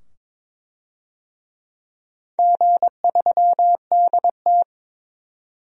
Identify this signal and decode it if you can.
Morse code, heard as a keyed tone, about 23 wpm, 700 Hz. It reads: G3DT
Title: DXCC-100 Call Signs - 4 Characters - 22wpm